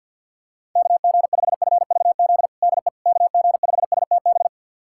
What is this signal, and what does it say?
Morse code, heard as a keyed tone, about 52 wpm, 700 Hz. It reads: KG5FVZ DE KG5STB
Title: USA Callsign DE Callsign - 6 Characters - 50wpm